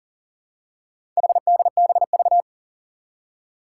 Morse code, heard as a keyed tone, about 40 wpm, 700 Hz. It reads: HDBV
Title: Four Challenging Letter-Number Combinations 40wpm